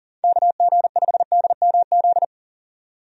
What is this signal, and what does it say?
Morse code, heard as a keyed tone, about 37 wpm, 700 Hz. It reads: KG5DMZ
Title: USA Call Signs - 6 Characters - 40wpm